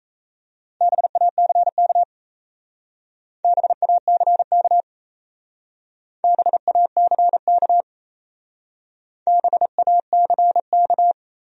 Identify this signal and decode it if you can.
Morse code, heard as a keyed tone, about 34 wpm, 700 Hz. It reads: BACK BACK BACK BACK
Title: Speed-Racing Top 500 Words 28wpm